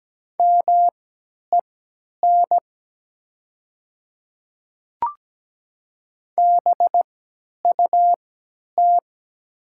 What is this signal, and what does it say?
Morse code, heard as a keyed tone, about 17 wpm, spaced as for 10 wpm, 690 Hz. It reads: MEN BUT
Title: 3x Character Spacing - Top 200 Words - 17wpm